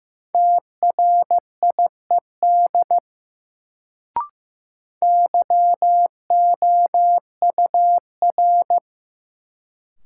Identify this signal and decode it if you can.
Morse code, heard as a keyed tone, about 15 wpm, 700 Hz. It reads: TRIED YOUR
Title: Sets of 2 Words - New Word Drill - 15wpm